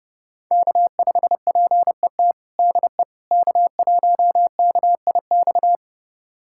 Morse code, heard as a keyed tone, about 30 wpm, 700 Hz. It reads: K5PET DE K1KIX
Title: USA Callsign DE Callsign - 5 Characters - 30wpm